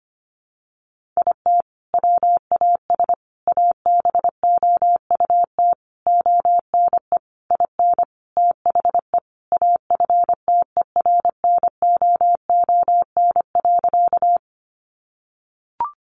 Morse code, heard as a keyed tone, about 25 wpm, 700 Hz. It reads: IT WAS ABOUT ONE IN THE AFTERNOON.
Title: Sentences from Top 1500 Words 25wpm